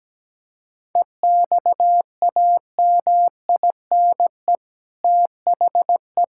Morse code, heard as a keyed tone, about 17 wpm, 700 Hz. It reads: EXAMINE THE
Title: Sentences from Top 1600 Words 17wpm